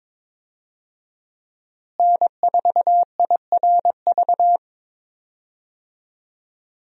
Morse code, heard as a keyed tone, about 22 wpm, 700 Hz. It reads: N4IRV
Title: USA Call Signs - 5 Characters - Repeated 2x - 22wpm